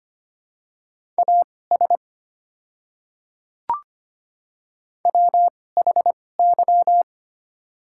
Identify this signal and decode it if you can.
Morse code, heard as a keyed tone, about 25 wpm, spaced as for 18 wpm, 700 Hz. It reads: AS WHY
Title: 2x Character Spacing - Top 300 Words - 25wpm